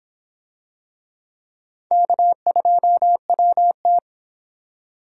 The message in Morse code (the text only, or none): K2WT